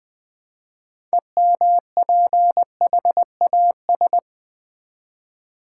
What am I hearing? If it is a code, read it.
Morse code, heard as a keyed tone, about 21 wpm, 700 Hz. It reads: EMPHAS
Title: Build A Word - 2 Letters At A Time - 8-Letter Words - 20wpm